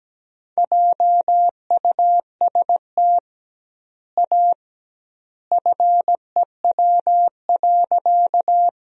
Morse code, heard as a keyed tone, about 17 wpm, 700 Hz. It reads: JUST A FEW.
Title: 2x Word-Spacing - Sentences from Top 300 Words - 17wpm